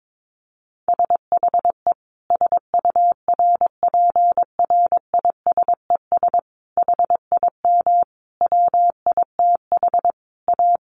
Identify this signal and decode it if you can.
Morse code, heard as a keyed tone, about 22 wpm, 700 Hz. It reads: SHE SURPRISES HIM WITH A
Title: Additional Easy Phrases 22wpm